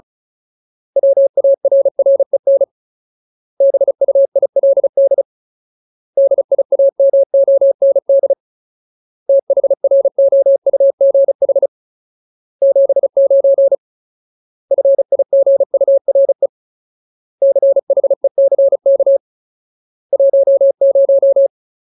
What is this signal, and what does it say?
Morse code, heard as a keyed tone, about 34 wpm, 550 Hz. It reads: WARREN BUILD DIAMOND THROUGH 79 FIGURE CHECK 10
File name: Ultimate Word List / 35WPM Ultimate